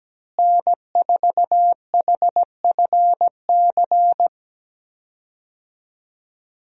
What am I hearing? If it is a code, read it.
Morse code, heard as a keyed tone, about 17 wpm, 700 Hz. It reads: N4HFC
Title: USA Call Signs - 5 Characters - Repeated 3x - 17wpm